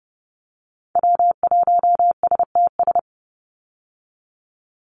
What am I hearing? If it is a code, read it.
Morse code, heard as a keyed tone, about 30 wpm, 700 Hz. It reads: W1STS
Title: USA Call Signs 30wpm